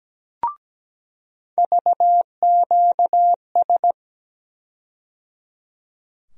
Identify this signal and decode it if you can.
Morse code, heard as a keyed tone, about 17 wpm, 700 Hz. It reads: VQS
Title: USA Call Sign Postfixes - 3 Letters 17wpm